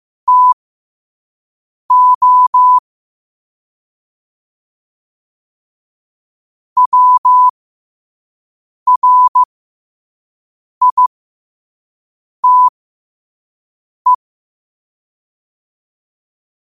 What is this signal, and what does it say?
Morse code, heard as a keyed tone, about 15 wpm, spaced as for 5 wpm, 1000 Hz. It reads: TO WRITE
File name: qotdfiles / 5wpm